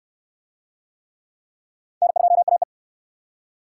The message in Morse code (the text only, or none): N2AE